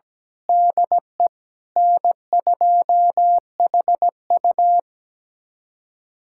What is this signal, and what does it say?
Morse code, heard as a keyed tone, about 17 wpm, 700 Hz. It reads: DE N2HU
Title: USA Callsign DE Callsign - 4 Characters - 17wpm